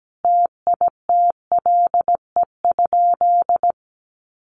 Morse code, heard as a keyed tone, about 17 wpm, 700 Hz. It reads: TITLE?
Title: Sentences from Top 1200 Words 17wpm